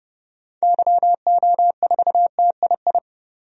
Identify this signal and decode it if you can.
Morse code, heard as a keyed tone, about 30 wpm, 700 Hz. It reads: YO4TII
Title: DXCC-100 Call Signs - 6 Characters - 30wpm